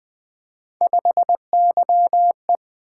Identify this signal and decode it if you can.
Morse code, heard as a keyed tone, about 20 wpm, 700 Hz. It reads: 5YE
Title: Three Letter-Number Combinations - 20wpm